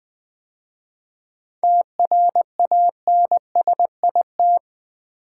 Morse code, heard as a keyed tone, about 20 wpm, 690 Hz. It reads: TRANSIT
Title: 7 Letter Words - First 3 Letters Given - 20wpm